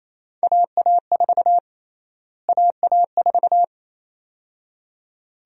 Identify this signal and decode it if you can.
Morse code, heard as a keyed tone, about 28 wpm, 700 Hz. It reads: AA4 AA4
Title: USA Call Sign Prefixes - Repeated 2x - 28wpm